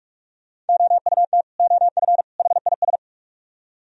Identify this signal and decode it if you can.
Morse code, heard as a keyed tone, about 43 wpm, 700 Hz. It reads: OUT OF HIS